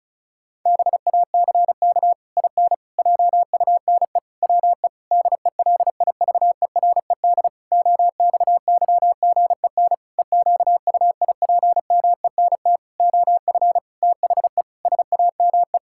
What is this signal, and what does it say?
Morse code, heard as a keyed tone, about 34 wpm, 700 Hz. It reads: BACK IN JUNE WE DELIVERED OXYGEN EQUIPMENT OF THE SAME
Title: Pangram Challenge v2 - 35wpm